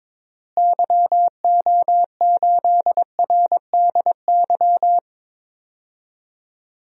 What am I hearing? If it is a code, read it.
Morse code, heard as a keyed tone, about 22 wpm, 700 Hz. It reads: YO8RDY
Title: DXCC-100 Call Signs - 6 Characters - Repeated 3x - 22wpm